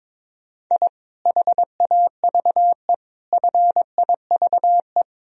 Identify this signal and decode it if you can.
Morse code, heard as a keyed tone, about 22 wpm, 700 Hz. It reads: I HAVE FIVE
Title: Sentences from Top 1000 Words - Review Part 2 - 22wpm